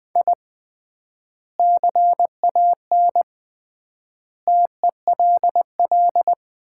Morse code, heard as a keyed tone, about 20 wpm, 700 Hz. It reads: I CAN TELL
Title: 3x Word-Spacing - Sets of 5 Words - 20wpm